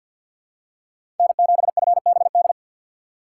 Morse code, heard as a keyed tone, about 51 wpm, 690 Hz. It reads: N7FBD
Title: USA Call Signs - 6x Character Spacing - 50wpm